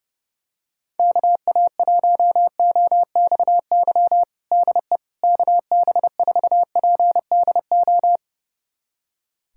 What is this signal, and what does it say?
Morse code, heard as a keyed tone, about 32 wpm, 700 Hz. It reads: KA1OXY DE KB4PDO
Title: USA Callsign DE Callsign - 6 Characters - 30wpm